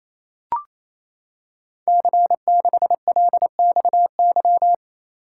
Morse code, heard as a keyed tone, about 27 wpm, 700 Hz. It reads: C6LXY
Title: Code Groups - 5 Letter:Numbers - 28wpm